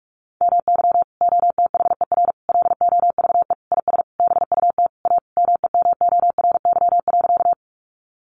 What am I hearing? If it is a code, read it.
Morse code, heard as a keyed tone, about 44 wpm, 700 Hz. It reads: MY OTHER LOVE IS BUT A MEMORY.
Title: Sentences from Top 1300 Words 45wpm